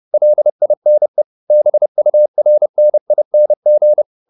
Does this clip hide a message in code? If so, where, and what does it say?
Yes: Morse code from 0.1 to 4.0 s, heard as a keyed tone, about 30 wpm, 600 Hz. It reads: LINE BURNING